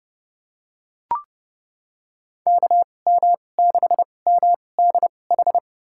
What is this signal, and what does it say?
Morse code, heard as a keyed tone, about 32 wpm, spaced as for 22 wpm, 700 Hz. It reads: KM6MDH
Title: USA Call Signs - 6 Characters - 2x Character Spacing - 30wpm